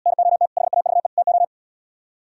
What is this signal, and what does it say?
Morse code, heard as a keyed tone, about 77 wpm, 700 Hz. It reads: NOT HAVE TO